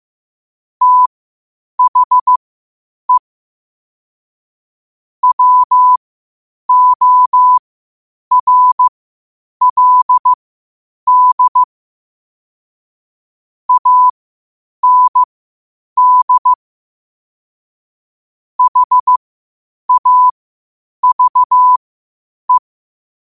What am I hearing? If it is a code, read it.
Morse code, heard as a keyed tone, about 15 wpm, spaced as for 8 wpm, 1000 Hz. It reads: THE WORLD AND HAVE
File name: qotdfiles / 8wpm